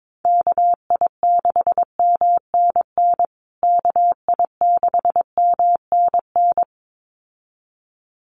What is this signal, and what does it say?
Morse code, heard as a keyed tone, about 22 wpm, 700 Hz. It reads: KI6MNN KI6MNN
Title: USA Call Signs Repeated 2x 22wpm